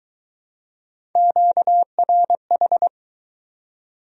Morse code, heard as a keyed tone, about 19 wpm, 700 Hz. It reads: QRH